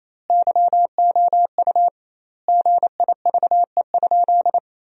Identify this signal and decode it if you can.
Morse code, heard as a keyed tone, about 28 wpm, 700 Hz. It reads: YOU GIVE?
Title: 2x Word-Spacing - Sentences from Top 200 Words - 28wpm